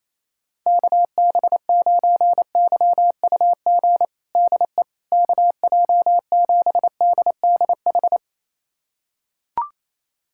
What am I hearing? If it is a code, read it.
Morse code, heard as a keyed tone, about 28 wpm, 700 Hz. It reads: KB9YUG DE KJ7DDH
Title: USA Callsign DE Callsign - 6 Characters - 28wpm